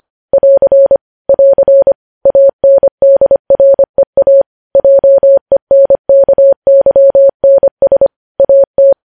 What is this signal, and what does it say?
Morse code, heard as a keyed tone, about 25 wpm, 550 Hz. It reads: <AR> <AR> ANDREA JENKYNS AT
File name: podcast / News headlines-1x-25x25